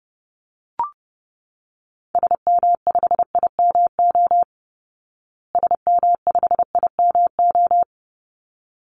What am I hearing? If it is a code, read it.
Morse code, heard as a keyed tone, about 30 wpm, 700 Hz. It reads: SM5IMO SM5IMO
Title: Call Signs Repeated 3x - 30wpm